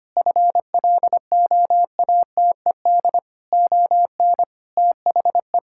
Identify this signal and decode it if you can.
Morse code, heard as a keyed tone, about 25 wpm, 700 Hz. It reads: FLOATED ON THE
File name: practice / 25wpm / Sentences with 4-Word Phrases - 25wpm